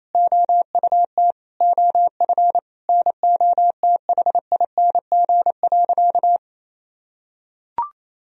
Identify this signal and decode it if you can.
Morse code, heard as a keyed tone, about 28 wpm, 700 Hz. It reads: OUT OF NOTHING.
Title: Sentences from Top 1000 Words - Review Part 3 - 28wpm